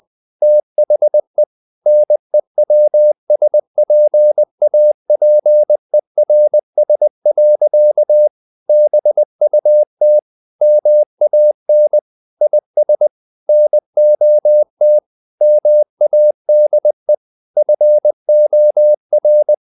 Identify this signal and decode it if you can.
Morse code, heard as a keyed tone, about 20 wpm, 600 Hz. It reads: THE NEWSPAPERS. BUT MAN IS NOT MADE FOR
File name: the old man and the sea part 2 20wpm